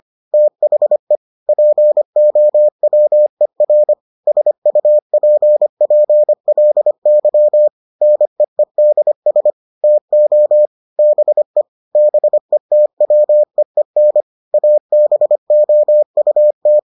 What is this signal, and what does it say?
Morse code, heard as a keyed tone, about 25 wpm, 600 Hz. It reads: THE POWER SUPPLY NEEDS TO BE BETWEEN ABOUT